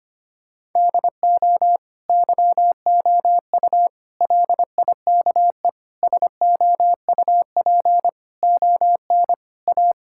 Morse code, heard as a keyed tone, about 25 wpm, 700 Hz. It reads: DO YOU LIKE SOUP ON A
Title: Sentences from Top 1000 Words - Review Part 6 - 25wpm